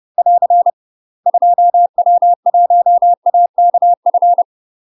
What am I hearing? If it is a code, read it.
Morse code, heard as a keyed tone, about 30 wpm, 700 Hz. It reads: <AR> 2W1AKF
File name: UK MASTER QSOs TRAINING / 30WPM / UK MASTER QSOs 0049